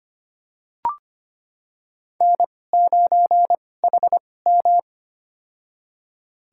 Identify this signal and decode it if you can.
Morse code, heard as a keyed tone, about 25 wpm, spaced as for 18 wpm, 700 Hz. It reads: N9HM
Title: USA Call Signs - 4 Characters - 2x Character Spacing - 25wpm